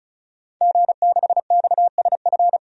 Morse code, heard as a keyed tone, about 36 wpm, 700 Hz. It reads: G6XSF
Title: DXCC-100 Call Signs - 5 Characters - 35wpm